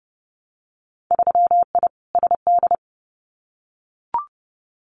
Morse code, heard as a keyed tone, about 30 wpm, 700 Hz. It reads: 3I SD
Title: Winter Field Day Exchanges Encore 30wpm